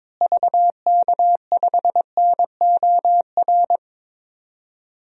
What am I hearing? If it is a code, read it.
Morse code, heard as a keyed tone, about 21 wpm, 700 Hz. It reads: VK5NOR
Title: DXCC-100 Call Signs - 6 Characters - Repeated 3x - 22wpm